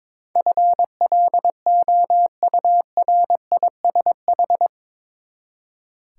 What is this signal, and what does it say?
Morse code, heard as a keyed tone, about 22 wpm, 700 Hz. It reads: FLOURISH
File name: practice / 22wpm / Eight Letter Words - 22wpm